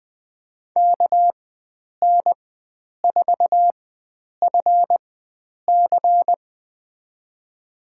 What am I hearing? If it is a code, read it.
Morse code, heard as a keyed tone, about 20 wpm, spaced as for 9 wpm, 700 Hz. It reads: KN4FC
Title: USA Call Signs - 5 Characters - 4x Character Spacing - 20wpm